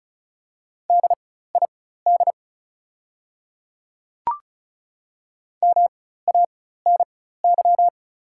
Morse code, heard as a keyed tone, about 34 wpm, spaced as for 16 wpm, 700 Hz. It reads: DID MANY
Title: 4x Character Spacing - Top 200 Words - 35wpm